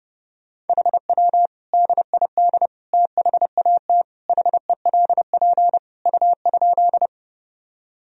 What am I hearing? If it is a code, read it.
Morse code, heard as a keyed tone, about 30 wpm, 700 Hz. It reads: HW DID THAT HELP U?
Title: Sentences from Top 200 Words and Abbreviations 30wpm